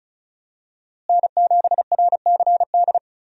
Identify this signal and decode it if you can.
Morse code, heard as a keyed tone, about 34 wpm, 700 Hz. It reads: N7RCD
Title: USA Call Signs - 5 Characters - Repeated 3x - 35wpm